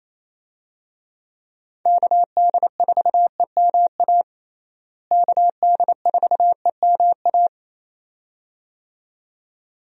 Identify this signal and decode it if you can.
Morse code, heard as a keyed tone, about 27 wpm, 700 Hz. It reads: KD4EMA KD4EMA
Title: USA Call Signs - 6 Characters - Repeated 2x - 28wpm